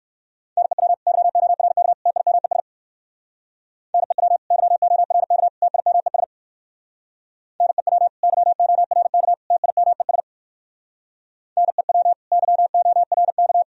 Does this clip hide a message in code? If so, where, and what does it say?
Yes: Morse code from 0.6 to 13.6 s, heard as a keyed tone, about 57 wpm, 700 Hz. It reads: NEW YORK TIMES NEW YORK TIMES NEW YORK TIMES NEW YORK